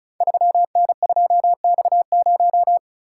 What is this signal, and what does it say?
Morse code, heard as a keyed tone, about 34 wpm, 700 Hz. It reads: 3N2X0
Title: Code Groups - 5 Letter:Numbers - 35wpm